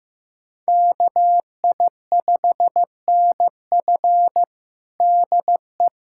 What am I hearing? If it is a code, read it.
Morse code, heard as a keyed tone, about 15 wpm, 700 Hz. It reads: KI5NF DE
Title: USA Callsign DE Callsign - 5 Characters - 15wpm